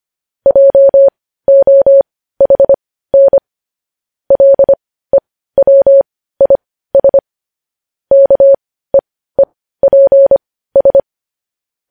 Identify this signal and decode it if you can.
Morse code, heard as a keyed tone, about 25 wpm, spaced as for 15 wpm, 550 Hz. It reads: JOHN LEWIS KEEPS
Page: news-headlines-in-morse-code-at-15-wpm